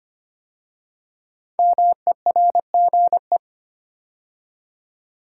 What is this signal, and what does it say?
Morse code, heard as a keyed tone, about 25 wpm, 700 Hz. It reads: MERGE
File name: 5 Letter Words - First 3 Letters Given - 25wpm